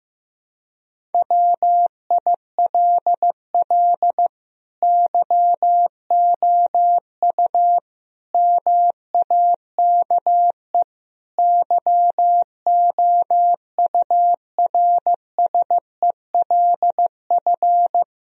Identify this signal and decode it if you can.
Morse code, heard as a keyed tone, about 15 wpm, 700 Hz. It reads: WILL YOU MAKE YOURSELF
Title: Sentences from Top 1000 Words - Review Part 3 - 15wpm